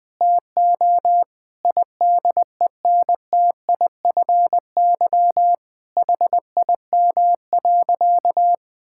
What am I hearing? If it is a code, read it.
Morse code, heard as a keyed tone, about 20 wpm, 700 Hz. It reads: TO IDENTIFY HIM.
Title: Sentences from Top 1000 Words - Review Part 6 - 20wpm